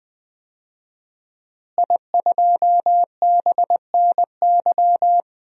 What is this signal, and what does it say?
Morse code, heard as a keyed tone, about 21 wpm, 700 Hz. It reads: I2BNY
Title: DXCC-100 Call Signs - 5 Characters - Repeated 3x - 20wpm